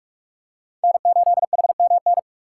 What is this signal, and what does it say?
Morse code, heard as a keyed tone, about 46 wpm, 700 Hz. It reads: N8HMN
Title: USA Call Signs - 5 Characters - 3x Character Spacing - 45wpm